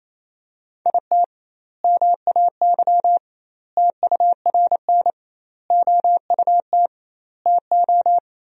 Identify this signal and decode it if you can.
Morse code, heard as a keyed tone, about 28 wpm, 700 Hz. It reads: IT MAY TURN OUT TO
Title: 2x Word-Spacing - Sentences from Top 400 Words - 28wpm